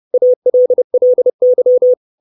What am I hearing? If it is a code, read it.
Morse code, heard as a keyed tone, about 30 wpm, 500 Hz. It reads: ALLY
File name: 632 Words 30WPM